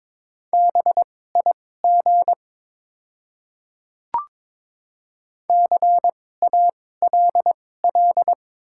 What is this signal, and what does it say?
Morse code, heard as a keyed tone, about 22 wpm, spaced as for 16 wpm, 700 Hz. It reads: BIG CALL
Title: 2x Character Spacing - Top 200 Words - 22wpm